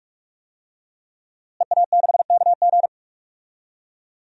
Morse code, heard as a keyed tone, about 42 wpm, 700 Hz. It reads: EA6KG
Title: DXCC-100 Call Signs - 5 Characters - Repeated 2x - 45wpm